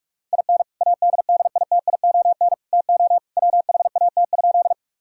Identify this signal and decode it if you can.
Morse code, heard as a keyed tone, about 46 wpm, 690 Hz. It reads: IN ADDITION TO WHAT?
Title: Sentences from Top 1000 Words - Review Part 6 - 45wpm